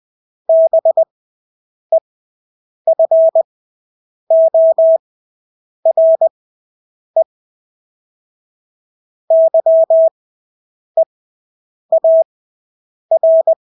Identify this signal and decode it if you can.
Morse code, heard as a keyed tone, about 20 wpm, spaced as for 8 wpm, 650 Hz. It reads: BEFORE YEAR
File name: Engelske ord 20/8 wpm